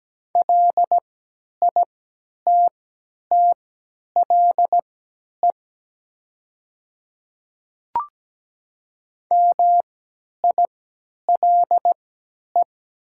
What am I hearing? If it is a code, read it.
Morse code, heard as a keyed tone, about 17 wpm, spaced as for 10 wpm, 700 Hz. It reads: LITTLE MILE
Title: 3x Character Spacing - Top 300 Words - 17wpm